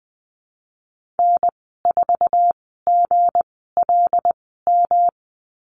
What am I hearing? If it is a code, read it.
Morse code, heard as a keyed tone, about 20 wpm, spaced as for 14 wpm, 700 Hz. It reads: N4GLM
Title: 161 Lesson 26g - Letter g - Call Signs 20wpm